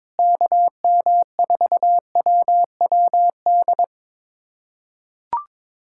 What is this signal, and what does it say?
Morse code, heard as a keyed tone, about 22 wpm, 700 Hz. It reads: KM4WWD
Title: USA Callsign DE Callsign - 6 Characters - 22wpm